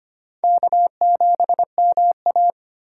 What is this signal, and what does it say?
Morse code, heard as a keyed tone, about 31 wpm, 700 Hz. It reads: K7MA
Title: Speed Racing - USA 1x2 Call Signs 25wpm